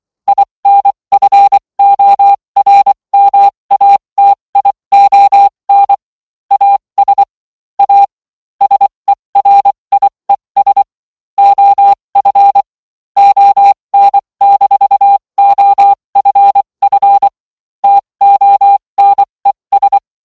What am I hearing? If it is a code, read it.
Morse code, heard as a keyed tone, about 24 wpm, spaced as for 20 wpm, 800 Hz. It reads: INFORMATION AS A SERIES OF ON-OFF TONES